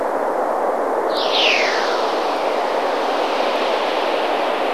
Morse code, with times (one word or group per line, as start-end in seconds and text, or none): none